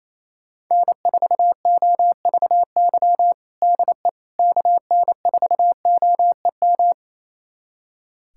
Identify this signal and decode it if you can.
Morse code, heard as a keyed tone, about 28 wpm, 700 Hz. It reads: N4OVY DE KN4OEM
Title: USA Callsign DE Callsign 28wpm